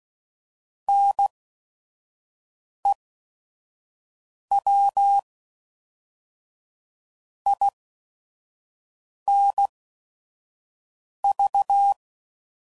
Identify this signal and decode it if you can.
Morse code, heard as a keyed tone, about 16 wpm, spaced as for 5 wpm, 800 Hz. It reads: NEW INV
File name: CW Code Groups @5WPM / QSOs / WD0FBT 2